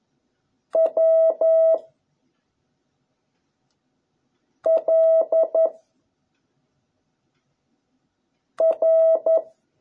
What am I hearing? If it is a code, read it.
Morse code, heard as a keyed tone, about 11 wpm, 660 Hz. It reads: W L R